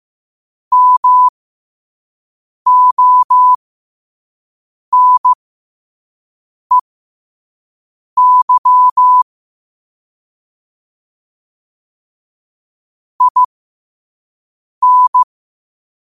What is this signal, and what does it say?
Morse code, heard as a keyed tone, about 15 wpm, spaced as for 5 wpm, 1000 Hz. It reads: MONEY IN
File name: qotdfiles / 5wpm